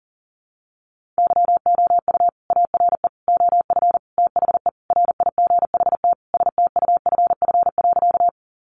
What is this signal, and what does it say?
Morse code, heard as a keyed tone, about 40 wpm, 700 Hz. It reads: YOU ARE OF THE RIGHT STUFF.
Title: Sentences from Top 1400 Words 40wpm